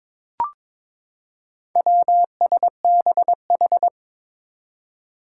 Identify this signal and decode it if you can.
Morse code, heard as a keyed tone, about 23 wpm, 700 Hz. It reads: WSBH
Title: Four Challenging Letter-Number Combinations 22wpm